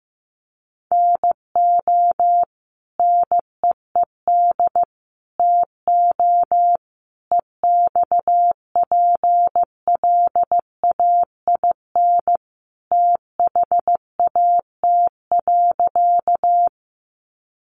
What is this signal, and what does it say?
Morse code, heard as a keyed tone, about 15 wpm, 700 Hz. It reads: NO NEED TO EXPLAIN THAT.
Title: Sentences from Top 1100 Words 15wpm